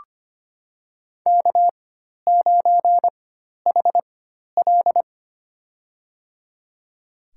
Morse code, heard as a keyed tone, about 25 wpm, spaced as for 12 wpm, 700 Hz. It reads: K9HL
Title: USA Call Signs - 4 Characters - 4x Character Spacing - 25wpm